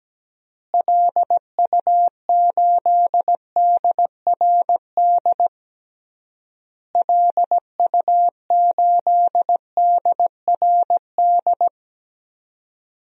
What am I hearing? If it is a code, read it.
Morse code, heard as a keyed tone, about 17 wpm, 700 Hz. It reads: LU8DRD LU8DRD
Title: DXCC-100 Call Signs - 6 Characters - Repeated 3x - 17wpm